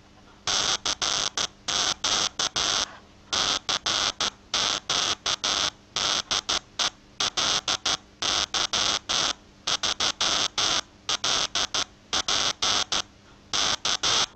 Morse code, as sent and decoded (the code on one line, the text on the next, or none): none